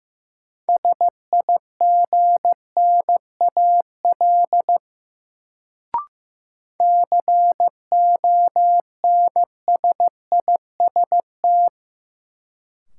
Morse code, heard as a keyed tone, about 15 wpm, 700 Hz. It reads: SIGNAL CONSIST
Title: Sentences from Top 1400 Words - New Word Drill - 15wpm